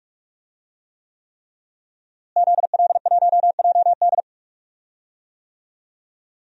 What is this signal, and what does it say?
Morse code, heard as a keyed tone, about 41 wpm, 700 Hz. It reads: ZL1JD
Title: DXCC-100 Call Signs - 5 Characters - Repeated 3x - 45wpm